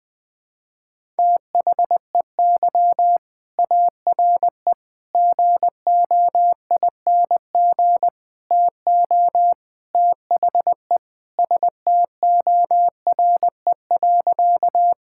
Morse code, heard as a keyed tone, about 21 wpm, 700 Hz. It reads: THEY ARE GOING TO THE STORE.
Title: Sentences with 4-Word Phrases - 20wpm